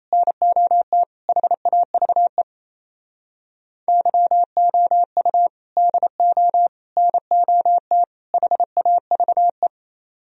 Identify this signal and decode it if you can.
Morse code, heard as a keyed tone, about 36 wpm, 700 Hz. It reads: NOT HAVE YOU DO NOT HAVE